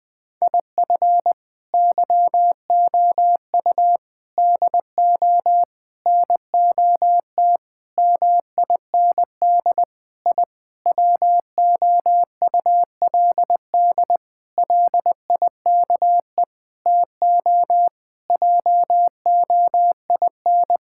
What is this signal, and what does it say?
Morse code, heard as a keyed tone, about 20 wpm, 700 Hz. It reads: IF YOU DO NOT MIND I WOULD LIKE TO JOIN
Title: Sentences with 5-Word Phrases - 20wpm